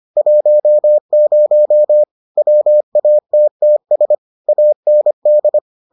Morse code, heard as a keyed tone, about 25 wpm, 600 Hz. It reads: 10 WATTS AND